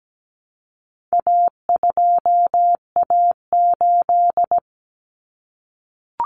Morse code, heard as a keyed tone, about 17 wpm, 700 Hz. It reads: A2A8
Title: Common Cut Numbers - 4-Digit 17wpm